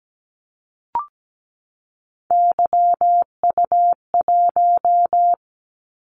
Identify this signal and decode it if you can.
Morse code, heard as a keyed tone, about 17 wpm, 700 Hz. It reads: YU1
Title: Callsign Prefixes for Top 25 DXCC - 17wpm